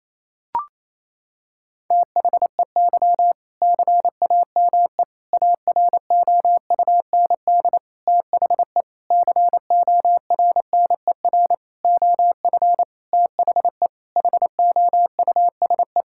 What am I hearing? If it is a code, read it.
Morse code, heard as a keyed tone, about 27 wpm, 700 Hz. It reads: THEY CAME AROUND THE CORNER OF THE HOUSE
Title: Sentences from Top 1600 Words 28wpm